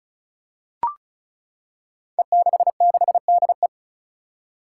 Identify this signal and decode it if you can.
Morse code, heard as a keyed tone, about 36 wpm, 700 Hz. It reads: E66DE
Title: Five Challenging Letter-Number Combinations 35wpm